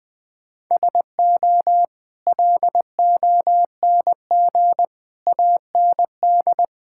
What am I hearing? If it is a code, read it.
Morse code, heard as a keyed tone, about 21 wpm, 700 Hz. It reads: SO LONG AND
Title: Sentences with 8-Word Phrases - 20wpm